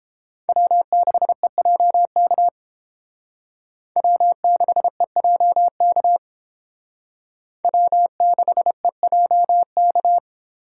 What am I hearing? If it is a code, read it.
Morse code, heard as a keyed tone, about 32 wpm, 700 Hz. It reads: W6EJK W6EJK W6EJK